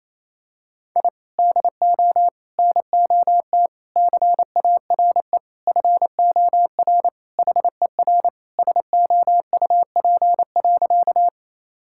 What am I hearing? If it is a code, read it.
Morse code, heard as a keyed tone, about 27 wpm, 700 Hz. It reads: I DO NOT CARE FOR HER SOUP.
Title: Sentences from Top 1000 Words - Review Part 6 - 28wpm